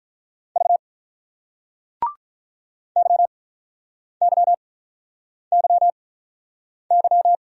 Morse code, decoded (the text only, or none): V YYYY